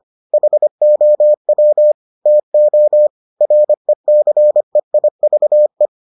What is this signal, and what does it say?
Morse code, heard as a keyed tone, about 25 wpm, 600 Hz. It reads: HOW TO RECEIVE